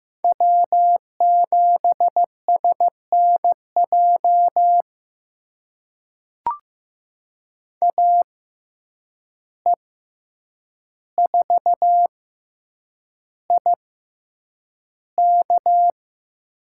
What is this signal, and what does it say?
Morse code, heard as a keyed tone, about 15 wpm, spaced as for 8 wpm, 700 Hz. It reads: W7SNJ AE4IK
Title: USA Call Signs - 5 Characters - 6x Character Spacing - 15wpm